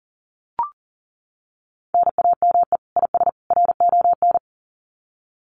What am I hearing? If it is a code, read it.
Morse code, heard as a keyed tone, about 40 wpm, 700 Hz. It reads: NAME IS RON
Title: Common QSO Phrases 40wpm